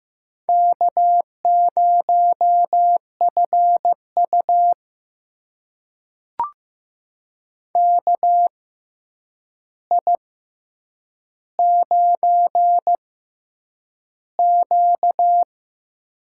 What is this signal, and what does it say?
Morse code, heard as a keyed tone, about 15 wpm, spaced as for 8 wpm, 700 Hz. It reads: K0FU KI9Q
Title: USA Call Signs - 4 Characters - 6x Character Spacing - 15wpm